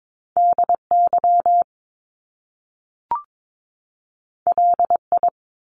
Morse code, heard as a keyed tone, about 22 wpm, 700 Hz. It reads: DY LI
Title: Top Two Letter Combinations in Words 22wpm